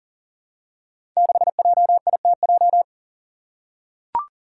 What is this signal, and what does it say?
Morse code, heard as a keyed tone, about 38 wpm, 700 Hz. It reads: 6JITJ
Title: Five Challenging Letter-Number Combinations 40wpm